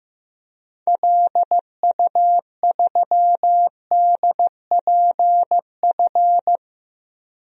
Morse code, heard as a keyed tone, about 15 wpm, 700 Hz. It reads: LU3DPF
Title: DXCC-100 Call Signs - 6 Characters - 15wpm